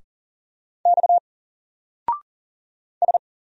Morse code, heard as a keyed tone, about 40 wpm, 700 Hz. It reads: X S
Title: Single Character (Letter:Number:Punctuation:Prosign) Alt 40wpm